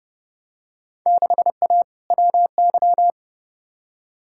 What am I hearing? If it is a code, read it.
Morse code, heard as a keyed tone, about 31 wpm, 700 Hz. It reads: 6A WY
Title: Field Day Exchanges Encore - 30wpm